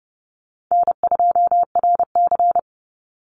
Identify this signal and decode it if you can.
Morse code, heard as a keyed tone, about 32 wpm, 700 Hz. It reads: N2RC
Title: Easy Call Signs - 30wpm